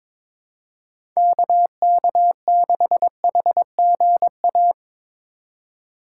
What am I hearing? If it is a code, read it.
Morse code, heard as a keyed tone, about 22 wpm, 700 Hz. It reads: KK6HGA
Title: USA Call Signs - 6 Characters - 6x Character Spacing - 22wpm